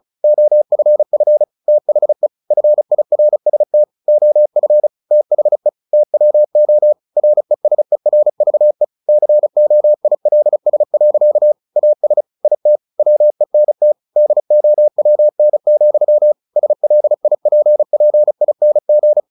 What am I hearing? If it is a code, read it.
Morse code, heard as a keyed tone, about 34 wpm, 600 Hz. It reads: OFF THE FIRST OF THE TWO RESERVE COILS. AS IT WENT DOWN, SLIPPING
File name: the old man and the sea / the old man and the sea 1 35wpm